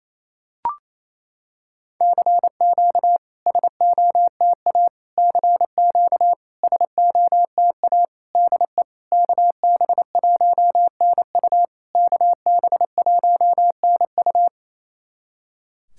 Morse code, heard as a keyed tone, about 28 wpm, 700 Hz. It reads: CQ SOTA CQ SOTA DE KB1NU KB1NU
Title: SOTA QSO Practice 28wpm